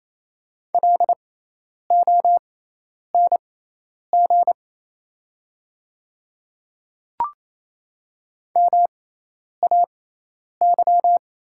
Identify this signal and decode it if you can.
Morse code, heard as a keyed tone, about 28 wpm, spaced as for 10 wpm, 690 Hz. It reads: LONG MAY
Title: 6x Character Spacing - Top 100 Words - 28wpm